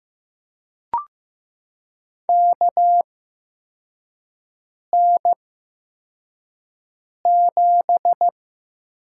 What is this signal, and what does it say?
Morse code, heard as a keyed tone, about 15 wpm, spaced as for 4 wpm, 700 Hz. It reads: KN7
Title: USA Call Signs - 8x Character Spacing - 15wpm